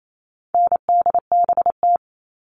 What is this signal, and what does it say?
Morse code, heard as a keyed tone, about 28 wpm, 700 Hz. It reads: NDBT
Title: All Cut Numbers - 4-Digit 28wpm